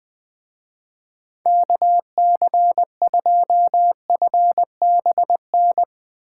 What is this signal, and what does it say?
Morse code, heard as a keyed tone, about 21 wpm, 700 Hz. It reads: KC2FBN
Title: Speed Racing - USA 2x3 Call Signs 20wpm